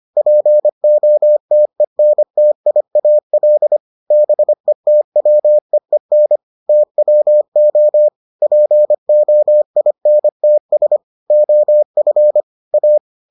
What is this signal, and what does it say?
Morse code, heard as a keyed tone, about 25 wpm, 600 Hz. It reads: POTENTIAL BETWEEN TWO POINTS OF A